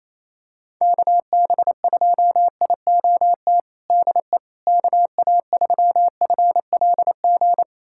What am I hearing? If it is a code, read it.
Morse code, heard as a keyed tone, about 28 wpm, 700 Hz. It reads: KB2IOT DE KA3FLG
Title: USA Callsign DE Callsign - 6 Characters - 28wpm